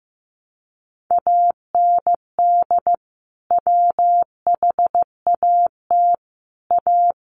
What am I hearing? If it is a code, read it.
Morse code, heard as a keyed tone, about 15 wpm, 700 Hz. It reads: AND WHAT A
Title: Sentences from Top 600 Words 15wpm